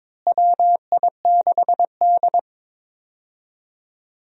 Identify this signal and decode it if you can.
Morse code, heard as a keyed tone, about 22 wpm, 700 Hz. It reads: WI6D
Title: USA Call Signs - 4 Characters - Repeated 3x - 22wpm